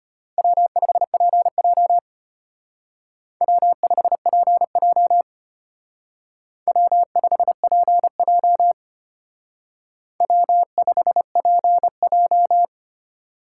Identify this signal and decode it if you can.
Morse code, heard as a keyed tone, about 32 wpm, 700 Hz. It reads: W5PJ W5PJ W5PJ W5PJ